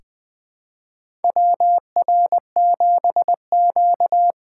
Morse code, heard as a keyed tone, about 20 wpm, 700 Hz. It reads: WR7Q
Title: USA Call Signs - 4 Characters - 4x Character Spacing - 20wpm